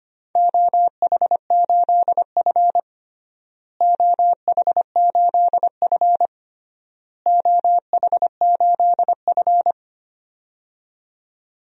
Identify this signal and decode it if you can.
Morse code, heard as a keyed tone, about 25 wpm, 700 Hz. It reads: OH8F OH8F OH8F
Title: DXCC-100 Call Signs - 4 Characters - Repeated 3x - 25wpm